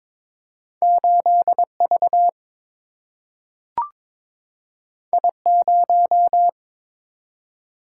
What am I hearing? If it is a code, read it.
Morse code, heard as a keyed tone, about 22 wpm, 700 Hz. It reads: 8 V I 0
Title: Single-Letter Single-Number 22wpm